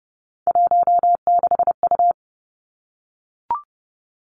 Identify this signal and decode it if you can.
Morse code, heard as a keyed tone, about 30 wpm, 700 Hz. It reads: 16U
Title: Three Challenging Letter-Number Combinations 30wpm